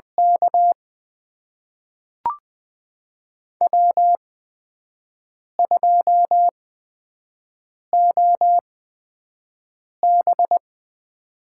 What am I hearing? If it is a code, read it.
Morse code, heard as a keyed tone, about 20 wpm, spaced as for 5 wpm, 700 Hz. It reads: K W2OB
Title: USA Call Signs - 4 Characters - 8x Character Spacing - 20wpm